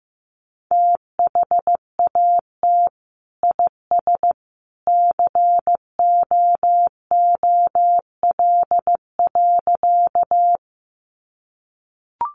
Morse code, heard as a keyed tone, about 15 wpm, 700 Hz. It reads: THAT IS COOL.
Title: Sentences from Top 1200 Words 15wpm